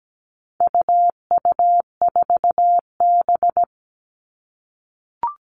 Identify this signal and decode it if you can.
Morse code, heard as a keyed tone, about 17 wpm, 700 Hz. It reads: UU4B
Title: All Cut Numbers - 4-Digit 17wpm